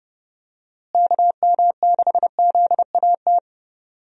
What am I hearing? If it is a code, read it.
Morse code, heard as a keyed tone, about 32 wpm, 700 Hz. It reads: KM6ZAT
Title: USA Call Signs - 6 Characters - Repeated 2x - 30wpm